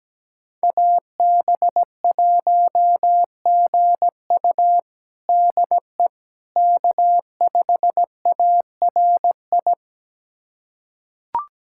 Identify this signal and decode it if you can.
Morse code, heard as a keyed tone, about 17 wpm, 700 Hz. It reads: AB1GU DE K5ARI
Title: USA Callsign DE Callsign - 5 Characters - 17wpm